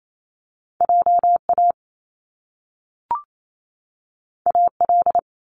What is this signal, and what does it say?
Morse code, heard as a keyed tone, about 28 wpm, 700 Hz. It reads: JA AL
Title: Top Two Letter Combinations in Words 28wpm